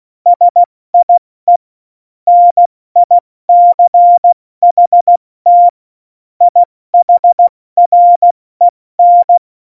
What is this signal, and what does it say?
Morse code, heard as a keyed tone, about 16 wpm, 700 Hz. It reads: SIE NICHT IHREN
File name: Morsekurs / Witze / Witz 19. Juni 2018, 70 F80